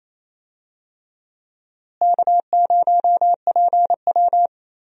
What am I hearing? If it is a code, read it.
Morse code, heard as a keyed tone, about 28 wpm, 700 Hz. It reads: K0PW
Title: USA Call Signs - 4 Characters - Repeated 3x - 28wpm